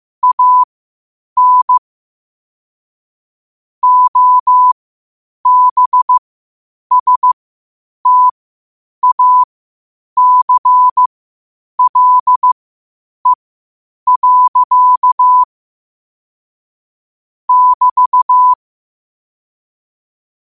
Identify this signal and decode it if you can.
Morse code, heard as a keyed tone, about 15 wpm, spaced as for 8 wpm, 1000 Hz. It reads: AN OBSTACLE. <BT>
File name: qotdfiles / 8wpm